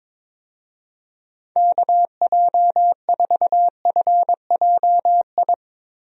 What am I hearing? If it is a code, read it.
Morse code, heard as a keyed tone, about 22 wpm, 700 Hz. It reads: KJ4FJI
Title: USA Call Signs - 6 Characters - Repeated 2x - 22wpm